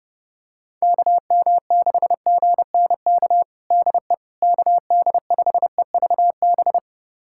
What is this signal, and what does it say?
Morse code, heard as a keyed tone, about 30 wpm, 700 Hz. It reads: KM6GNK DE KD5EVB
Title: USA Callsign DE Callsign - 6 Characters - 30wpm